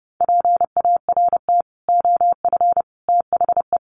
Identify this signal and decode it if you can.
Morse code, heard as a keyed tone, about 30 wpm, 700 Hz. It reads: PART OF THE
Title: Sets of 3 Words 30wpm